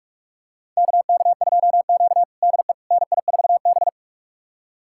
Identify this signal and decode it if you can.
Morse code, heard as a keyed tone, about 41 wpm, 700 Hz. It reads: KK2Q DE NI4B
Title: USA Callsign DE Callsign - 4 Characters - 45wpm